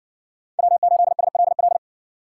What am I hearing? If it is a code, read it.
Morse code, heard as a keyed tone, about 57 wpm, 700 Hz. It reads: W8SLL